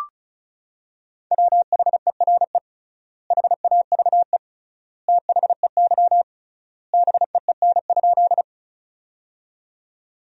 Morse code, heard as a keyed tone, about 36 wpm, 700 Hz. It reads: WHERE HAVE THEY BEEN?
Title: 3x Word-Spacing - Sentences from Top 200 Words - 35wpm